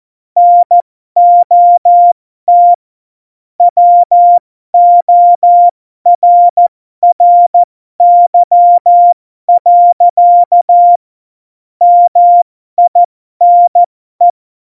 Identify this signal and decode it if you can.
Morse code, heard as a keyed tone, about 14 wpm, spaced as for 12 wpm, 700 Hz. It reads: NOT WORRY. MINE
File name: Morsekurs / Witze